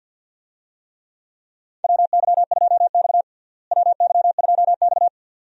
Morse code, heard as a keyed tone, about 52 wpm, 690 Hz. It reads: WY2X WY2X
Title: USA Call Signs - 4 Characters - Repeated 2x - 50wpm